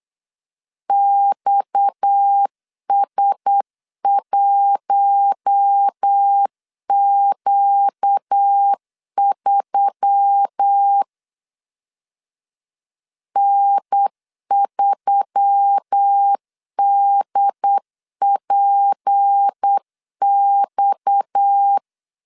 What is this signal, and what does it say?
Morse code, heard as a keyed tone, about 8 wpm, 800 Hz. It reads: XS1Q3 N3DPX
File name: LEZIONI / 6 PKX0 / 6 PKX 8wpm 3041